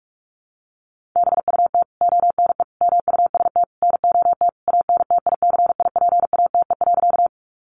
Morse code, heard as a keyed tone, about 44 wpm, 700 Hz. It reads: BUT ONE MUST NOT ANTICIPATE.
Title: Sentences from Top 1500 Words 45wpm